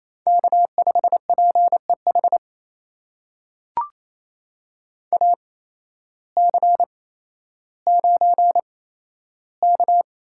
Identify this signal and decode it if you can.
Morse code, heard as a keyed tone, about 27 wpm, spaced as for 13 wpm, 700 Hz. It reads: K5PEH AC9K
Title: USA Call Signs - 8x Character Spacing - 28wpm